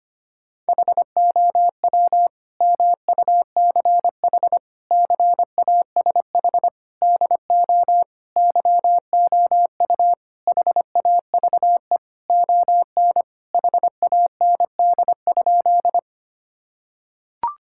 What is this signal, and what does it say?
Morse code, heard as a keyed tone, about 25 wpm, 700 Hz. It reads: HOW MUCH CASH DO YOU HAVE ON HAND?
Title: Sentences from Top 1000 Words - Review Part 5 - 25wpm